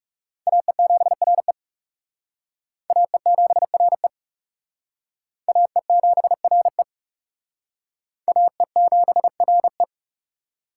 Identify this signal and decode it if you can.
Morse code, heard as a keyed tone, about 35 wpm, 700 Hz. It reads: AE7RE AE7RE AE7RE AE7RE
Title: Speed Racing - USA 2x2 Call Signs 30wpm